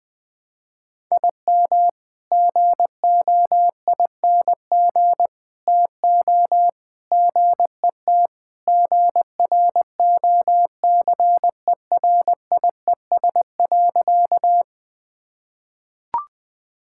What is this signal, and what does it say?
Morse code, heard as a keyed tone, about 20 wpm, 700 Hz. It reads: IM GOING TO GET GROCERIES.
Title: Sentences with 3-Word Phrases - 20wpm